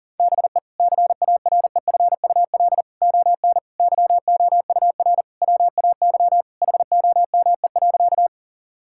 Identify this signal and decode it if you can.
Morse code, heard as a keyed tone, about 40 wpm, 700 Hz. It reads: BE CAREFUL ON YOUR WAY HOME.
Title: Sentences from Top 1600 Words 40wpm